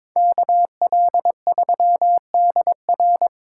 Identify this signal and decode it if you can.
Morse code, heard as a keyed tone, about 26 wpm, 700 Hz. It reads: KL3DR
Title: Speed Racing - USA 2x2 Call Signs 22wpm